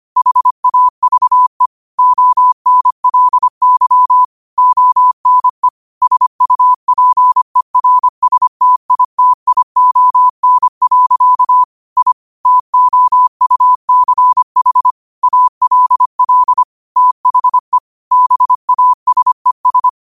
Morse code, heard as a keyed tone, about 25 wpm, 1000 Hz. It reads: HAVE ONLY ONE SUPERSTITION. I TOUCH ALL THE BASES
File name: qotdfiles / 25wpm